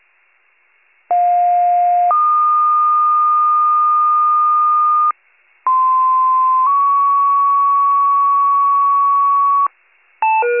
Two-tone page, heard as a one-second tone, then a three-second tone, about 710 then 1220 Hz, about 1040 then 1120 Hz. It is heard from 1.1 to 9.7 s.
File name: pager / 95 RESCUE / wcalert